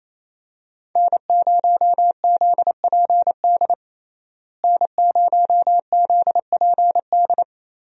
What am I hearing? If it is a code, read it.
Morse code, heard as a keyed tone, about 27 wpm, 700 Hz. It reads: N0ZPD N0ZPD
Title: USA Call Signs - 5 Characters - Repeated 3x - 28wpm